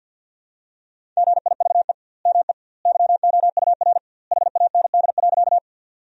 Morse code, heard as a keyed tone, about 50 wpm, 700 Hz. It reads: GIVE ME YOUR HAND.
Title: 2x Word-Spacing - Sentences from Top 200 Words - 50wpm